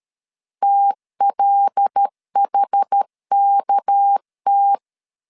Morse code, heard as a keyed tone, about 13 wpm, 800 Hz. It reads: TLHKT
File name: LEZIONI / 6 PKX0 / 6 PKX 12wpm 2024